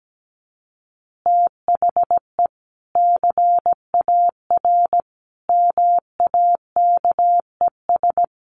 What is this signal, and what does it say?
Morse code, heard as a keyed tone, about 17 wpm, 700 Hz. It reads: THE CAR MAKES
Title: Easy Phrases 17wpm